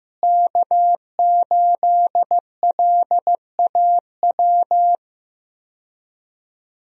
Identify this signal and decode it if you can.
Morse code, heard as a keyed tone, about 15 wpm, 700 Hz. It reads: K8LAW
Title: USA Call Signs - 5 Characters - Repeated 3x - 15wpm